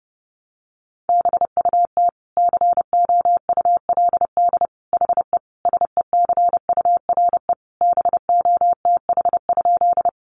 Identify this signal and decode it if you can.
Morse code, heard as a keyed tone, about 30 wpm, 700 Hz. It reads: BUT COULD HE SECURE BOTH?
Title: Sentences from Top 1200 Words 30wpm